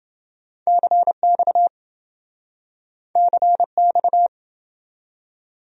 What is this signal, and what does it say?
Morse code, heard as a keyed tone, about 27 wpm, 700 Hz. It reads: CX CX